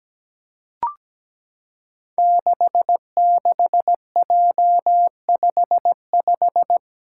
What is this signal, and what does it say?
Morse code, heard as a keyed tone, about 17 wpm, 700 Hz. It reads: 66J55
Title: Five Challenging Letter-Number Combinations 17wpm